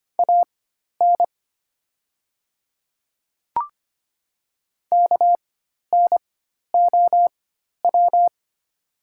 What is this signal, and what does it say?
Morse code, heard as a keyed tone, about 25 wpm, spaced as for 12 wpm, 700 Hz. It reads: AN KNOW
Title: 4x Character Spacing - Top 100 Words - 25wpm